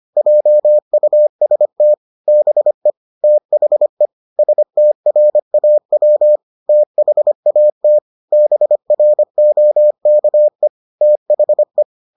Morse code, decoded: JUST BE THE STRAW THAT BROKE THE

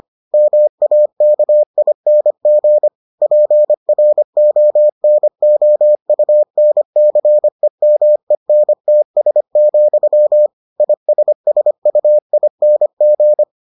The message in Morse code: MAKING PRONOUNCEMENTS, ISSUING

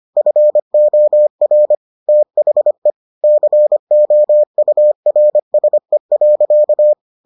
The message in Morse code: FOR THE COURSE.

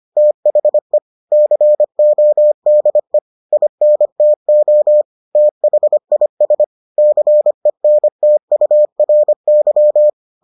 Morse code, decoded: THE CODE INTO THIS CENTURY

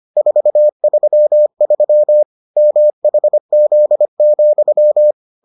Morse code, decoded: 433 MHZ,

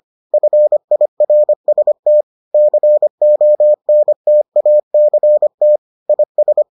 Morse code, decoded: FIRST CONTACT IS

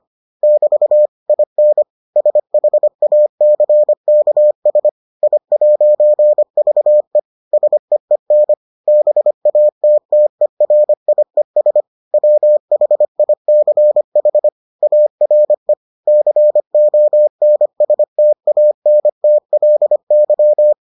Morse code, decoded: <BT> IN SHACKS I'VE SEEN BATTERIES WHICH ARE CONSTANTLY

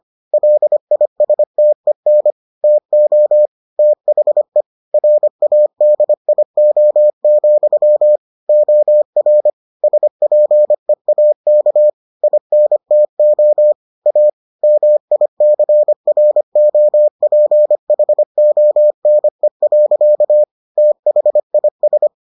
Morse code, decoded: LISTEN TO THE RADIO, OR SPEAK INTO A MICROPHONE. THIS

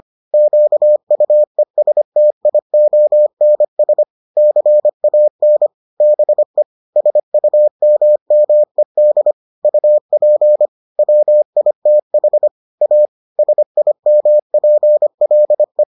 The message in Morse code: QUESTIONS CAN BE SUMMED UP WITH A SIMPLE

